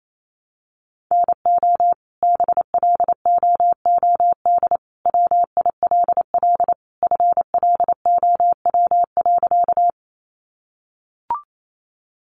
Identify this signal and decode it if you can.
Morse code, heard as a keyed tone, about 28 wpm, 700 Hz. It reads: NO BLOOD WILL FLOW.
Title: Sentences from Top 1200 Words 28wpm